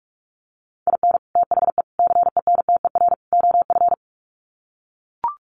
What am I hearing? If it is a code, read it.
Morse code, heard as a keyed tone, about 44 wpm, 700 Hz. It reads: IN THE CENTER OF
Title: Sets of 4 Words 45wpm